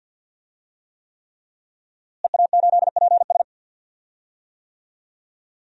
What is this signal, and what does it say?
Morse code, heard as a keyed tone, about 47 wpm, 700 Hz. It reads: EA8PS
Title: DXCC-100 Call Signs - 5 Characters - Repeated 3x - 50wpm